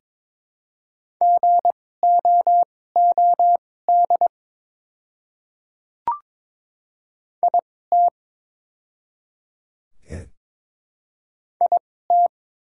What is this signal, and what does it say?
Morse code, heard as a keyed tone, about 22 wpm, spaced as for 16 wpm, 700 Hz. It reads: GOOD IT IT
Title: 2x Character Spacing - Top 200 Words - 22wpm